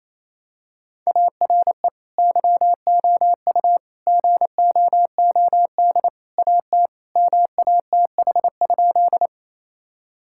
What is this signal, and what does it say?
Morse code, heard as a keyed tone, about 28 wpm, 700 Hz. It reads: ARE YOU GOOD AT MATH?
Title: Sentences from Top 1000 Words - Review Part 6 - 28wpm